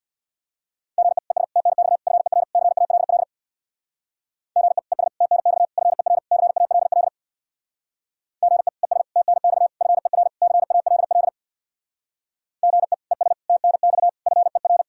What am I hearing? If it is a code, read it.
Morse code, heard as a keyed tone, about 64 wpm, 700 Hz. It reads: GE ES TNX FER CALL GE ES TNX FER CALL GE ES TNX FER CALL GE ES TNX FER